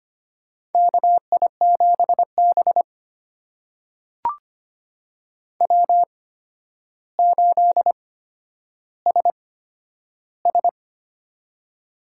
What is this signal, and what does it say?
Morse code, heard as a keyed tone, about 25 wpm, spaced as for 11 wpm, 700 Hz. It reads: KI7B W8SS
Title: USA Call Signs - 4 Characters - 8x Character Spacing - 25wpm